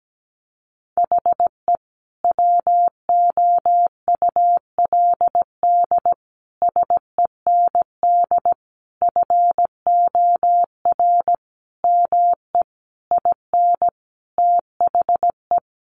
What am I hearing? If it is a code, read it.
Morse code, heard as a keyed tone, about 17 wpm, 700 Hz. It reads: HE WOULD SEND FOR ME IN THE
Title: Sentences from Top 1500 Words 17wpm